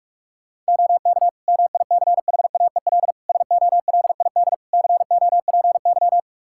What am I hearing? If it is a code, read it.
Morse code, heard as a keyed tone, about 41 wpm, 700 Hz. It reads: OK MICHAEL SOLID COPY
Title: Practice QSO - Exchange 3 - More Info - 45wpm